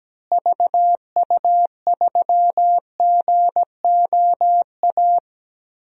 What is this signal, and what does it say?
Morse code, heard as a keyed tone, about 17 wpm, 700 Hz. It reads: VU3GOA
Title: DXCC-100 Call Signs - 6 Characters - Repeated 2x - 17wpm